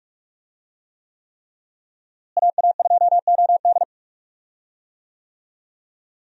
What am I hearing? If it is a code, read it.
Morse code, heard as a keyed tone, about 45 wpm, 700 Hz. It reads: AA2OD
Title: USA Call Signs - 5 Characters - Repeated 3x - 45wpm